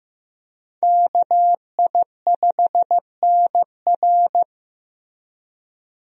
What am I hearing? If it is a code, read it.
Morse code, heard as a keyed tone, about 15 wpm, 700 Hz. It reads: KI5NR
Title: USA Call Signs - 5 Characters - Repeated 2x - 15wpm